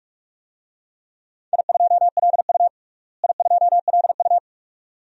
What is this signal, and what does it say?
Morse code, heard as a keyed tone, about 42 wpm, 700 Hz. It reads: I2LU I2LU
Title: DXCC-100 Call Signs - 4 Characters - Repeated 2x - 45wpm